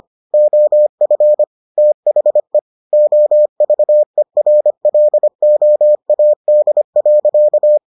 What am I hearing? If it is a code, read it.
Morse code, heard as a keyed tone, about 25 wpm, 600 Hz. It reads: OF THE OVERLOAD.